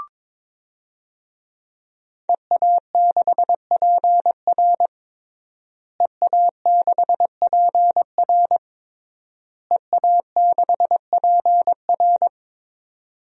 Morse code, heard as a keyed tone, about 22 wpm, 700 Hz. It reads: EA6PR EA6PR EA6PR
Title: DXCC-100 Call Signs - 5 Characters - Repeated 3x - 22wpm